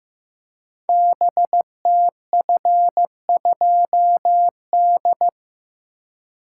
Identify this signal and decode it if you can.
Morse code, heard as a keyed tone, about 16 wpm, 700 Hz. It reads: BTF2D
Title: Code Groups - 5 Letter:Numbers - 15wpm